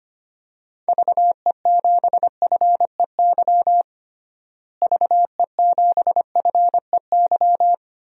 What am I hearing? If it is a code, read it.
Morse code, heard as a keyed tone, about 25 wpm, 700 Hz. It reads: VE7FEY VE7FEY
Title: DXCC-100 Call Signs - 6 Characters - Repeated 3x - 25wpm